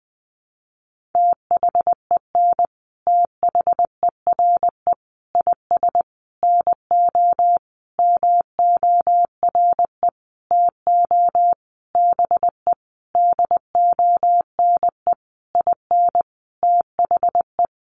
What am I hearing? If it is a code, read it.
Morse code, heard as a keyed tone, about 20 wpm, 700 Hz. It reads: THEN THERE IS NO MORE TO BE DONE IN THE
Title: Sentences from Top 700 Words 20wpm